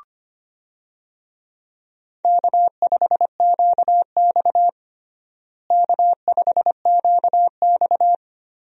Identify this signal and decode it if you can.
Morse code, heard as a keyed tone, about 25 wpm, 700 Hz. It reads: K5QX K5QX
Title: USA Call Signs - 4 Characters - Repeated 3x - 25wpm